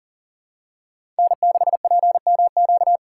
Morse code, heard as a keyed tone, about 40 wpm, 700 Hz. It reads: N6PMQ
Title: USA Call Signs - 3x Character Spacing - 40wpm